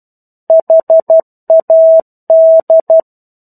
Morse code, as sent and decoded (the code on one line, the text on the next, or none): .... .- -..
HAD